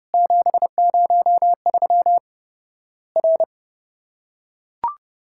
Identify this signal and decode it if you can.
Morse code, heard as a keyed tone, about 30 wpm, 700 Hz. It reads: 703 R
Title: Worked All Europe DX Contest - QTC Exchange - 30wpm